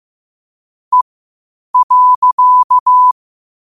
Morse code, heard as a keyed tone, about 15 wpm, spaced as for 8 wpm, 1000 Hz. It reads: E.
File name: qotdfiles / 8wpm